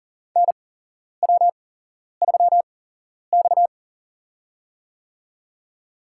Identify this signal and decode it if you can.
Morse code, heard as a keyed tone, about 40 wpm, spaced as for 11 wpm, 700 Hz. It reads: NW3X
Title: USA Call Signs - 4 Characters - 8x Character Spacing - 40wpm